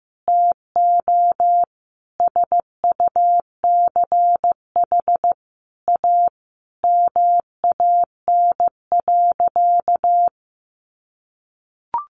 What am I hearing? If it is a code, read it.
Morse code, heard as a keyed tone, about 15 wpm, 700 Hz. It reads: TO SUCH A MAN.
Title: Sentences from Top 1300 Words 15wpm